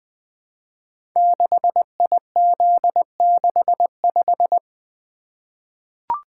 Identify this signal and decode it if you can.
Morse code, heard as a keyed tone, about 20 wpm, 700 Hz. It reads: 6IZ65
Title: Code Groups - 5 Letter:Numbers - 20wpm